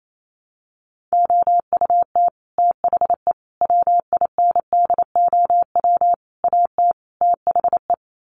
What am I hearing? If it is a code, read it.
Morse code, heard as a keyed tone, about 28 wpm, 700 Hz. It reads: OUT THE WINDOW AT THE
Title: Sets of 5 Words Encore2 28wpm